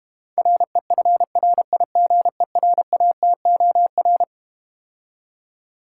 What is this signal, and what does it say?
Morse code, heard as a keyed tone, about 32 wpm, 700 Hz. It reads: REFRIGERATOR